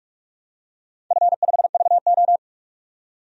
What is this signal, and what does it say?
Morse code, heard as a keyed tone, about 44 wpm, 700 Hz. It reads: F5VO
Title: DXCC-100 Call Signs - 4 Characters - 45wpm